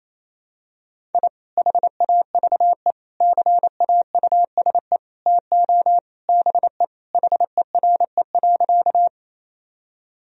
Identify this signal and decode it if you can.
Morse code, heard as a keyed tone, about 28 wpm, 700 Hz. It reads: I HAVE CAUSE TO BE HERE.
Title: Sentences from Top 1000 Words - Review Part 1 - 28wpm